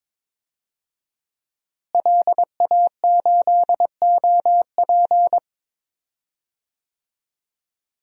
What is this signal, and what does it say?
Morse code, heard as a keyed tone, about 22 wpm, 700 Hz. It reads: LA8OP
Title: DXCC-100 Call Signs - 5 Characters - Repeated 3x - 22wpm